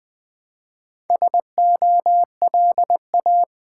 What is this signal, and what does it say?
Morse code, heard as a keyed tone, about 20 wpm, 700 Hz. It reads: SOLA
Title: Build A Word - 1 Letter At A Time - 3 to 5-Letter Words - 20wpm